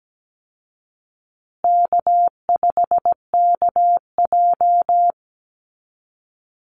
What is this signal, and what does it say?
Morse code, heard as a keyed tone, about 17 wpm, 700 Hz. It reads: K5KJ
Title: Call Signs Repeated 2x - 17wpm